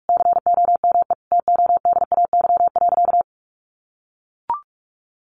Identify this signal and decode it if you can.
Morse code, heard as a keyed tone, about 44 wpm, 700 Hz. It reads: COME TODAY.
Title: Additional Easy Phrases 45wpm